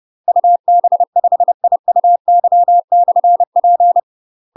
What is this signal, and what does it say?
Morse code, heard as a keyed tone, about 30 wpm, 700 Hz. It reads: UB5IUY/P
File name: UK MASTER QSOs TRAINING/30WPM